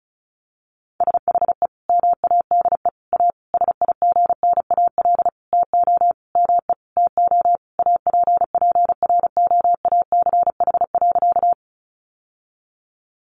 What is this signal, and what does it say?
Morse code, heard as a keyed tone, about 34 wpm, 700 Hz. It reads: SHE MADE A SIGNAL TO ME TO APPROACH.
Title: Sentences from Top 1400 Words 35wpm